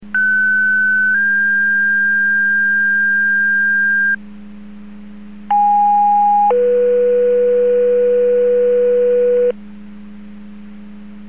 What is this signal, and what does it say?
Two-tone page, heard as a one-second tone, then a three-second tone, about 1500 then 1600 Hz, about 820 then 490 Hz. It is heard from 0.1 to 9.5 s.
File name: Portage 154.385 Paging